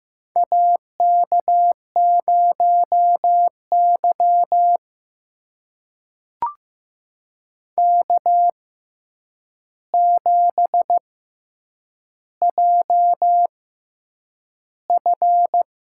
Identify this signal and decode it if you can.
Morse code, heard as a keyed tone, about 15 wpm, spaced as for 8 wpm, 700 Hz. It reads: AK0Y K7JF
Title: USA Call Signs - 4 Characters - 6x Character Spacing - 15wpm